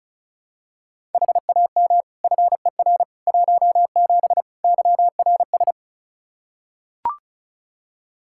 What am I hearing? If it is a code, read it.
Morse code, heard as a keyed tone, about 36 wpm, 700 Hz. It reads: HAM FER 17 YRS
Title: Practice QSO - Exchange 3 - More Info - 35wpm